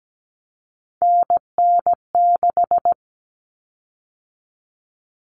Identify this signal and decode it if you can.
Morse code, heard as a keyed tone, about 17 wpm, 700 Hz. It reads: NN6
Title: Common Cut Numbers - 3-Digit 17wpm